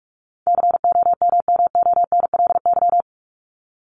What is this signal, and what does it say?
Morse code, heard as a keyed tone, about 44 wpm, 700 Hz. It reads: COMMONLY
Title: LY Suffix - Top 100 Words with Base Words 45wpm